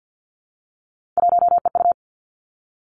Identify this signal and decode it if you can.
Morse code, heard as a keyed tone, about 50 wpm, 700 Hz. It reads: 1EU
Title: Three Challenging Letter-Number Combinations 50wpm